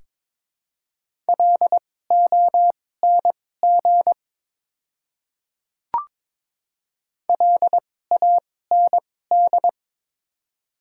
Morse code, heard as a keyed tone, about 22 wpm, spaced as for 16 wpm, 700 Hz. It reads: LONG LAND
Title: 2x Character Spacing - Top 200 Words - 22wpm